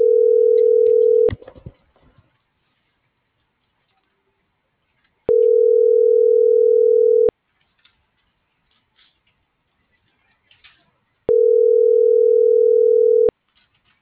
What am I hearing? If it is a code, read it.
Ringback tone (North America), 440 + 480 Hz.